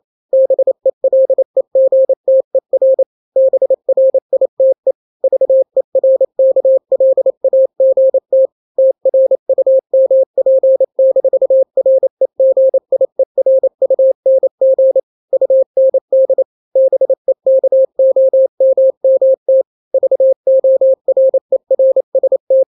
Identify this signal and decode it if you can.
Morse code, heard as a keyed tone, about 28 wpm, 530 Hz. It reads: BELEGTER BRITE VERKLAGT TRUMP-REGIERUNG UND BEKOMMT VORERST
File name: daily / 27wpm